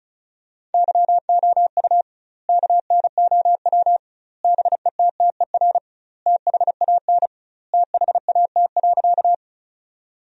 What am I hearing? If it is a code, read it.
Morse code, heard as a keyed tone, about 34 wpm, 700 Hz. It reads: YOU KNOW BETTER THAN THAT.
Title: 2x Word-Spacing - Sentences from Top 400 Words - 35wpm